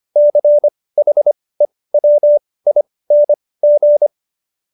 Morse code, heard as a keyed tone, about 25 wpm, spaced as for 18 wpm, 590 Hz. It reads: CHEWING